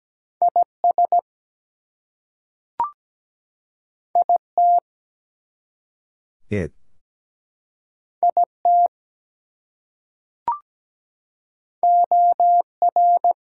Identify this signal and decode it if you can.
Morse code, heard as a keyed tone, about 17 wpm, 700 Hz. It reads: IS IT IT OR
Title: Two Letter Words - 17wpm